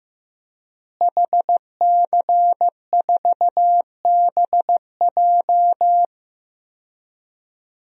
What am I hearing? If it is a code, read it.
Morse code, heard as a keyed tone, about 15 wpm, 700 Hz. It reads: HC4BJ
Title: DXCC-100 Call Signs - 5 Characters - Repeated 2x - 15wpm